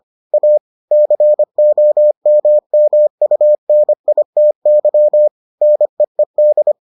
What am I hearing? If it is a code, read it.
Morse code, heard as a keyed tone, about 25 wpm, 600 Hz. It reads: A COMMUNITY NEED